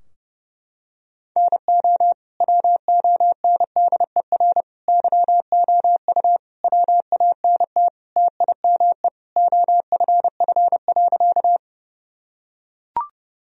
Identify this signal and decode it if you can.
Morse code, heard as a keyed tone, about 32 wpm, 700 Hz. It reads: NO WONDER YOU WANT TIME OFF.
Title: Sentences from Top 1000 Words - Review Part 3 - 30wpm